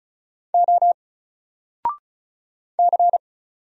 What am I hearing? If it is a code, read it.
Morse code, heard as a keyed tone, about 36 wpm, 700 Hz. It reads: O C
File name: Single-Letter-Number-35wpm